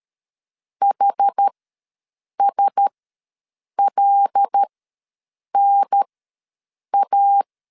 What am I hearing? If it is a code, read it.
Morse code, heard as a keyed tone, about 13 wpm, spaced as for 7 wpm, 800 Hz. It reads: HSLNA
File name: LEZIONI / 9 2 ws 12wpm 3406